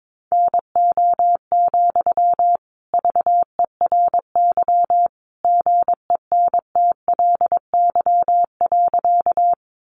Morse code, heard as a keyed tone, about 22 wpm, 700 Hz. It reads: NO, VERY GENTLY.
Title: Sentences from Top 1500 Words 22wpm